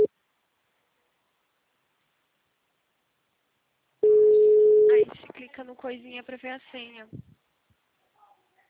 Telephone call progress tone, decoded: ringback tone

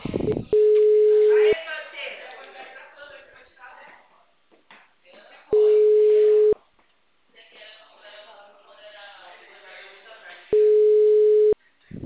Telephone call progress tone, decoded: ringback tone

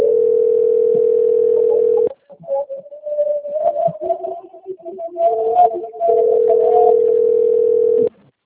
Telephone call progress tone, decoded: ringback tone

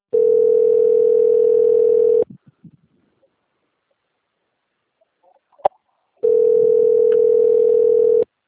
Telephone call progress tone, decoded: ringback tone

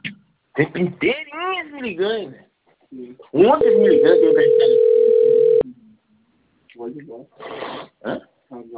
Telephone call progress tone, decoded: ringback tone

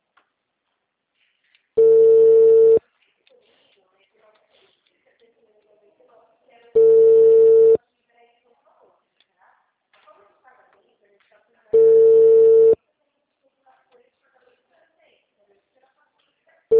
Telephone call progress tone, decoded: ringback tone

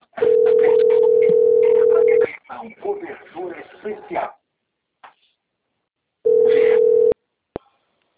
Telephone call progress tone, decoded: ringback tone

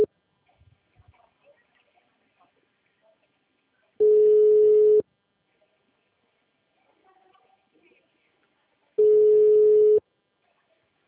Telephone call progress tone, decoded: ringback tone

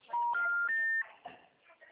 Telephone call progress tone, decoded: special information tone

